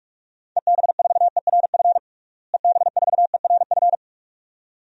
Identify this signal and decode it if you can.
Morse code, heard as a keyed tone, about 46 wpm, 700 Hz. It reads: EB4ERF EB4ERF
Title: DXCC-100 Call Signs - 6 Characters - Repeated 2x - 45wpm